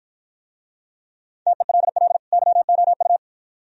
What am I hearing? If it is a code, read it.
Morse code, heard as a keyed tone, about 50 wpm, 700 Hz. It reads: TELL YOU